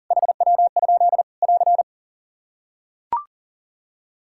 Morse code, heard as a keyed tone, about 37 wpm, 700 Hz. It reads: HW? <AR>
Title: Practice QSO - Exchange 2 - Info - 40wpm